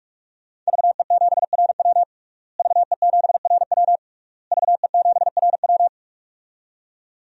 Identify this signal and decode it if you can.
Morse code, heard as a keyed tone, about 42 wpm, 700 Hz. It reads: VE7RW VE7RW VE7RW
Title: DXCC-100 Call Signs - 5 Characters - Repeated 3x - 45wpm